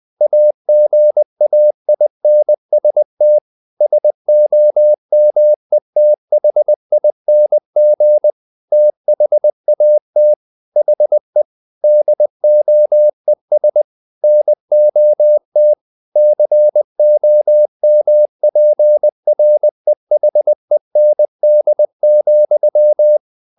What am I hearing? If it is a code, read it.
Morse code, heard as a keyed tone, about 20 wpm, 600 Hz. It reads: AGAINST SOMETHING THAT HE DOES NOT COMPREHEND,
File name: the old man and the sea part 2 20wpm